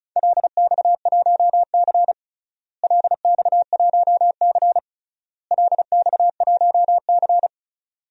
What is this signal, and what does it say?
Morse code, heard as a keyed tone, about 34 wpm, 700 Hz. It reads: LX1C LX1C LX1C
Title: DXCC-100 Call Signs - 4 Characters - Repeated 3x - 35wpm